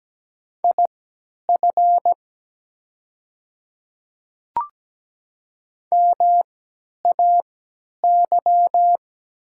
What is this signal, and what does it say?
Morse code, heard as a keyed tone, about 17 wpm, spaced as for 10 wpm, 700 Hz. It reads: IF MAY
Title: 3x Character Spacing - Top 300 Words - 17wpm